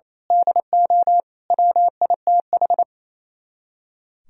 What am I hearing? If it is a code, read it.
Morse code, heard as a keyed tone, about 28 wpm, 700 Hz. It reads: DO WITH
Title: Speed-Racing - Sets of 5 Words 28wpm